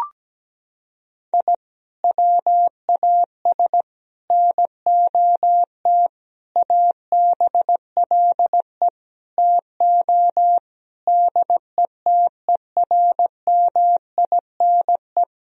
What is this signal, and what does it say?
Morse code, heard as a keyed tone, about 17 wpm, 700 Hz. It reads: I WAS NOT ABLE TO DETERMINE
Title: Sentences from Top 1000 Words - Review Part 4 - 17wpm